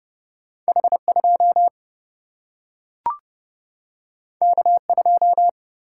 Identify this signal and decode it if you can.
Morse code, heard as a keyed tone, about 30 wpm, 700 Hz. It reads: H2 K2
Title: Single-Letter Single-Number 30wpm